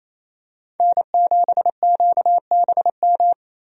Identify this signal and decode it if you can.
Morse code, heard as a keyed tone, about 28 wpm, 700 Hz. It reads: N7QBM
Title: USA Call Signs - 5 Characters - 28wpm